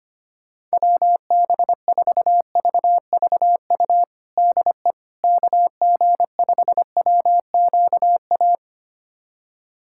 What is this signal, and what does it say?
Morse code, heard as a keyed tone, about 25 wpm, 700 Hz. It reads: WB4VVU DE KG5WQA
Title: USA Callsign DE Callsign - 6 Characters - 25wpm